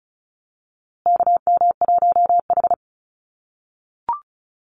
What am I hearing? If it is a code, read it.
Morse code, heard as a keyed tone, about 34 wpm, 700 Hz. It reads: KM1H
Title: USA Call Signs Repeated 3x 35wpm